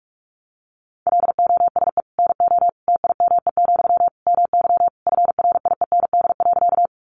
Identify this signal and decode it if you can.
Morse code, heard as a keyed tone, about 44 wpm, 700 Hz. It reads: LOSE NO TIME, MY FRIEND.
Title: Sentences from Top 1100 Words 45wpm